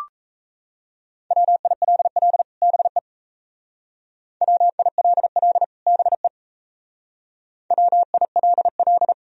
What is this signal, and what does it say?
Morse code, heard as a keyed tone, about 36 wpm, 700 Hz. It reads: WILL BE WILL BE WILL